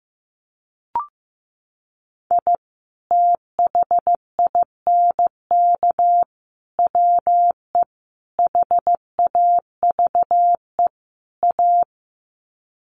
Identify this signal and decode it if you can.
Morse code, heard as a keyed tone, about 15 wpm, 700 Hz. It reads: I THINK WE HAVE A
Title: Sets of 5 Words Encore 15wpm